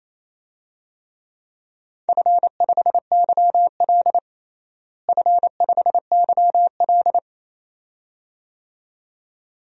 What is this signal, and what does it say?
Morse code, heard as a keyed tone, about 27 wpm, 700 Hz. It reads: F5YL F5YL
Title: DXCC-100 Call Signs - 4 Characters - Repeated 2x - 28wpm